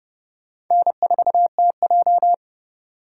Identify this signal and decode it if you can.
Morse code, heard as a keyed tone, about 34 wpm, 700 Hz. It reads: N4TJ
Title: Speed Racing - USA Call Signs 30wpm